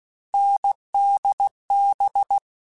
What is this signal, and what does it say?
Morse code, heard as a keyed tone, about 16 wpm, 800 Hz. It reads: NDB
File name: Difficult Combos / Difficult Combos-mp3s / awj1 ndb6 gz7 @15